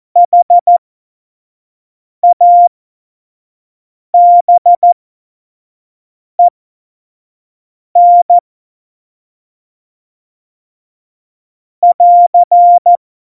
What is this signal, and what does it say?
Morse code, heard as a keyed tone, about 14 wpm, spaced as for 5 wpm, 700 Hz. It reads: HABEN <AR>
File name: Morsekurs / Witze / Witz 20. Oktober 2020, 25 F70